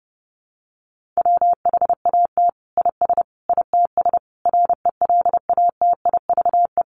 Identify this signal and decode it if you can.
Morse code, heard as a keyed tone, about 30 wpm, 700 Hz. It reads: WHAT IS ITS RELATIVE
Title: Sentences from Top 1400 Words 30wpm